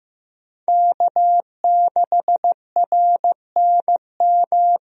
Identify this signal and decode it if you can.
Morse code, heard as a keyed tone, about 15 wpm, 700 Hz. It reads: K6RNM
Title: USA Call Signs - 5 Characters - Repeated 3x - 15wpm